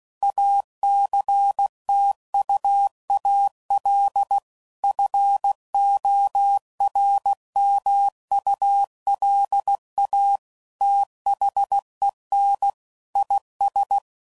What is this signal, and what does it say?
Morse code, heard as a keyed tone, about 16 wpm, 800 Hz. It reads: ACTUAL FORMULA THEN IS
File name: CW Code Groups @15WPM / Technical Subjects / XL